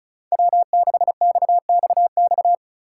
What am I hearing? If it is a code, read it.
Morse code, heard as a keyed tone, about 34 wpm, 700 Hz. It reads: W6XXX
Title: USA Call Signs - 5 Characters - 2x Character Spacing - 35wpm